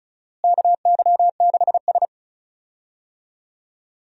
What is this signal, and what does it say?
Morse code, heard as a keyed tone, about 36 wpm, 700 Hz. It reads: KY6S
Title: USA Call Signs - 4 Characters - Repeated 2x - 35wpm